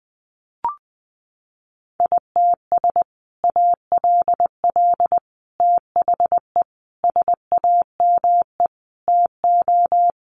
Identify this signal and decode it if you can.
Morse code, heard as a keyed tone, about 20 wpm, 700 Hz. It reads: ITS ALL THE SAME TO
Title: Sentences from Top 600 Words 20wpm